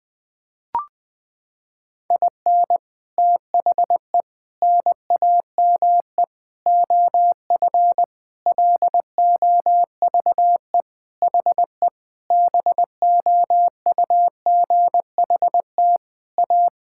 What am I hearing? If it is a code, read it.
Morse code, heard as a keyed tone, about 20 wpm, 700 Hz. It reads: IN THE NAME OF LOVE HE BOUGHT A
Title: Sentences with 4-Word Phrases - 20wpm